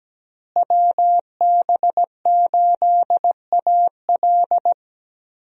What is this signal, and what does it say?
Morse code, heard as a keyed tone, about 17 wpm, 700 Hz. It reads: WB8AL
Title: USA Call Signs - 5 Characters - Repeated 2x - 17wpm